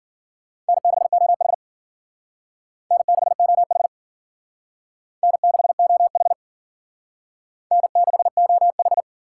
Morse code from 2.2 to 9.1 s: N6OH N6OH N6OH

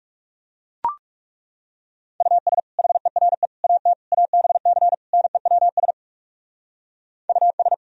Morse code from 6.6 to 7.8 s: US

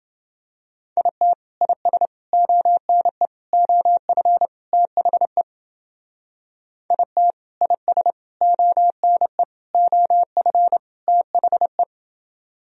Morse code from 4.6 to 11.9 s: THE IT IS ONE OF THE

